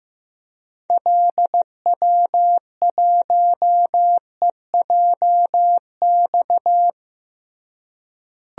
Morse code, heard as a keyed tone, about 15 wpm, 700 Hz. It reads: LW1EJX
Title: DXCC-100 Call Signs - 6 Characters - Repeated 2x - 15wpm